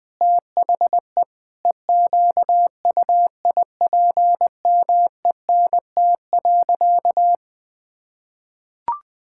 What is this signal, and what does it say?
Morse code, heard as a keyed tone, about 19 wpm, 700 Hz. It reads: THE EQUIPMENT.
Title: Sentences from Top 1000 Words - Review Part 5 - 20wpm